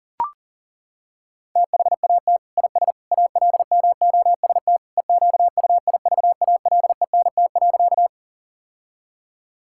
Morse code, heard as a keyed tone, about 40 wpm, 700 Hz. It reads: THAT IS ALMOST EQUIVALENT.
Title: Sentences from Top 1600 Words 40wpm